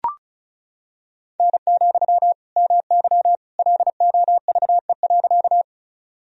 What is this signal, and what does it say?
Morse code, heard as a keyed tone, about 36 wpm, 700 Hz. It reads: N, MY LOVE.
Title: Sentences from Top 400 Words and Abbreviations 35wpm